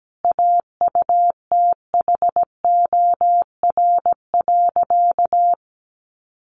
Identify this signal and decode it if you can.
Morse code, heard as a keyed tone, about 17 wpm, 700 Hz. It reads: AUTHOR.
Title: Sentences from Top 1200 Words 17wpm